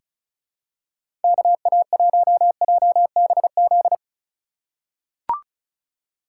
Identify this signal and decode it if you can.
Morse code, heard as a keyed tone, about 34 wpm, 700 Hz. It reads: KA1JBZ
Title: USA Call Signs - 2x Character Spacing - 35wpm